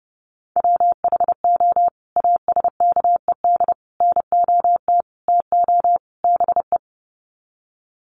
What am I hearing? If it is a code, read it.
Morse code, heard as a keyed tone, about 30 wpm, 700 Hz. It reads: WHO ASKED NOT TO BE
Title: Sets of 5 Words Encore2 30wpm